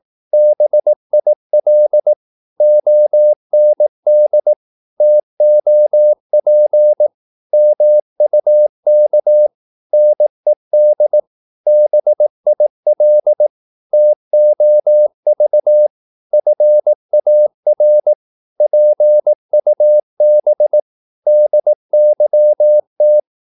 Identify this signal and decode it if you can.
Morse code, heard as a keyed tone, about 18 wpm, 600 Hz. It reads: BIL OND TOP MUK NED BIL TOV FAR PUB DYT
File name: Korte ord DK 18wpm